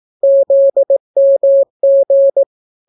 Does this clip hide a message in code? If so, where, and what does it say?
Yes: Morse code from 0.2 to 2.4 s, heard as a keyed tone, about 18 wpm, 550 Hz. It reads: ZMG